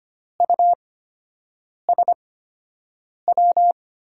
Morse code, heard as a keyed tone, about 25 wpm, spaced as for 7 wpm, 700 Hz. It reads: USW
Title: USA Call Signs - 6 Characters - 8x Character Spacing - 25wpm